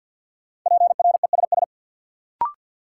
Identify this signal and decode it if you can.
Morse code, heard as a keyed tone, about 47 wpm, 700 Hz. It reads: PRESS
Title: Sentences from Top 800 Words - New Word Drill - 50wpm